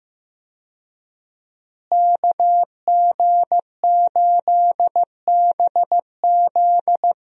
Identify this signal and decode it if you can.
Morse code, heard as a keyed tone, about 15 wpm, 700 Hz. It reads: KG8BZ
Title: USA Call Signs - 5 Characters - Repeated 2x - 15wpm